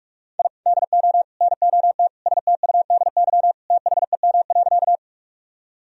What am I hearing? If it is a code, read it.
Morse code, heard as a keyed tone, about 46 wpm, 700 Hz. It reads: I DO NOT STUDY THEM.
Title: Sentences from Top 1000 Words - Review Part 1 - 45wpm